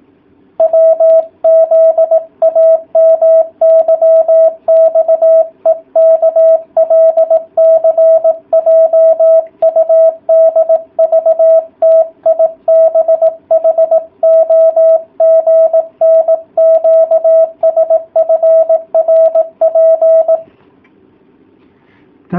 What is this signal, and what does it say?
Morse code, heard as a keyed tone, about 18 wpm, 650 Hz. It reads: WZAMYXEKLCJUDVTIBHOGNQSFRP